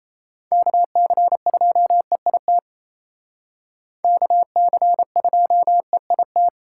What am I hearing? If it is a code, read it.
Morse code, heard as a keyed tone, about 33 wpm, 700 Hz. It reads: KC2EIT KC2EIT
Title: Speed Racing - USA 2x3 Call Signs 28wpm